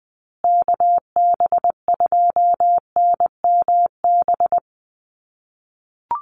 Morse code, heard as a keyed tone, about 20 wpm, 700 Hz. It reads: KB2NMB
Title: USA Call Signs Repeated 2x 20wpm